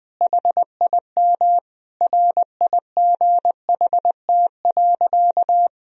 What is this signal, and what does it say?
Morse code, heard as a keyed tone, about 20 wpm, 700 Hz. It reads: HIM RIGHT.
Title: Sentences from Top 1000 Words - Review Part 2 - 20wpm